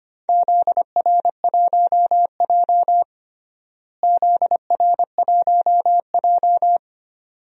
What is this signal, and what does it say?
Morse code, heard as a keyed tone, about 25 wpm, 700 Hz. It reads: ZR1J ZR1J
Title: DXCC-100 Call Signs - 4 Characters - Repeated 2x - 25wpm